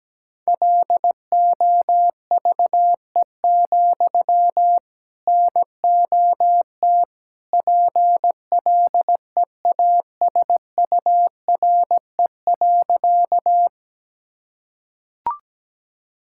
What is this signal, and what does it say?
Morse code, heard as a keyed tone, about 17 wpm, 700 Hz. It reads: LOVE, NOT PLEASURE.
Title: Sentences from Top 1600 Words 17wpm